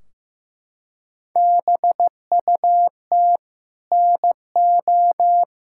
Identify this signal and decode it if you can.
Morse code, heard as a keyed tone, about 15 wpm, 700 Hz. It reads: BUT NO
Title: Sentences from Top 1600 Words 15wpm